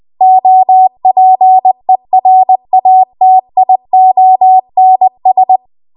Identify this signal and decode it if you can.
Morse code, heard as a keyed tone, about 20 wpm, 750 Hz. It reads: OPERATIONS
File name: Archive / 20 WPM / 20WPM